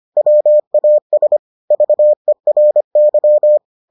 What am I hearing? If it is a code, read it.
Morse code, heard as a keyed tone, about 25 wpm, 600 Hz. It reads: WAS VERY